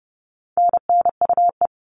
Morse code, heard as a keyed tone, about 30 wpm, 700 Hz. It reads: NNUE
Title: All Cut Numbers - 4-Digit 30wpm